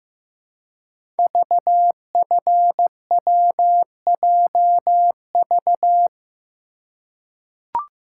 Morse code, heard as a keyed tone, about 15 wpm, 700 Hz. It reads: VFWJV
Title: Code Groups - 5 Letters - 15wpm